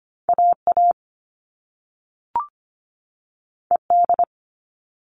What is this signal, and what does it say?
Morse code, heard as a keyed tone, about 25 wpm, 700 Hz. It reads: AA ED
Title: All Cut Numbers - 2-Digit 25wpm